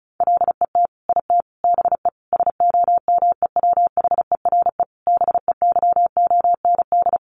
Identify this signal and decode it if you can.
Morse code, heard as a keyed tone, about 34 wpm, 700 Hz. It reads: LET IT BE SOMEWHERE BEYOND
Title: Sentences from Top 1400 Words 35wpm